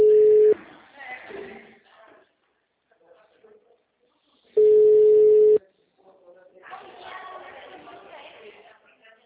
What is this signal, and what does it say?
Ringback tone (Europe, 425 Hz), with speech.
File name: out- -2014- - - .29523